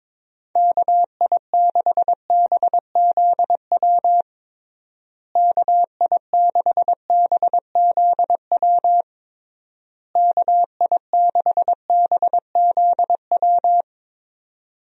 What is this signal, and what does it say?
Morse code, heard as a keyed tone, about 22 wpm, 700 Hz. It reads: KI6BZW KI6BZW KI6BZW
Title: USA Call Signs - 6 Characters - Repeated 3x - 22wpm